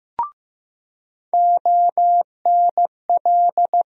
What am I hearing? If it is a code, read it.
Morse code, heard as a keyed tone, about 15 wpm, 700 Hz. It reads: ONL
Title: USA Call Sign Postfixes - 3 Letters 15wpm